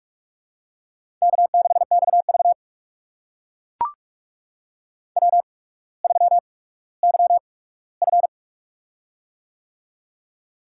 Morse code, heard as a keyed tone, about 44 wpm, spaced as for 19 wpm, 690 Hz. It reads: K6XV W3YF
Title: USA Call Signs - 4 Characters - 8x Character Spacing - 45wpm